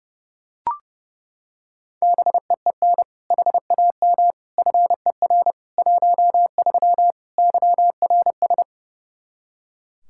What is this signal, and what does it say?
Morse code, heard as a keyed tone, about 30 wpm, 700 Hz. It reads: BEEN HAM FER 13 YRS
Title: Practice QSO - Exchange 3 - More Info - 30wpm